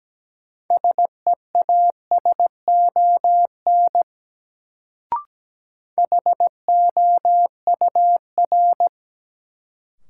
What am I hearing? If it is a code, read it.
Morse code, heard as a keyed tone, about 17 wpm, 700 Hz. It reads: SEASON HOUR
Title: Sets of 5 Words - New Word Drill - 17wpm